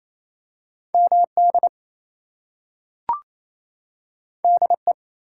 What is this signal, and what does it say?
Morse code, heard as a keyed tone, about 28 wpm, 700 Hz. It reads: MD DE
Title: US States and Canadian Provinces 28wpm